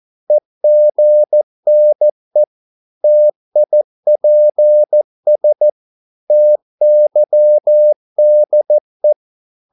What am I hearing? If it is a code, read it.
Morse code, heard as a keyed tone, about 14 wpm, 600 Hz. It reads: EGNE TIPS TYDE
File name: Korte ord DK 14wpm